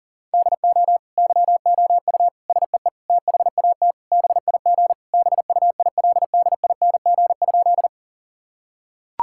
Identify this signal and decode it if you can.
Morse code, heard as a keyed tone, about 40 wpm, 700 Hz. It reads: DO YOU SEE THAT BIG BUILDING?
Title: Sentences from Top 1000 Words - Review Part 5 - 40wpm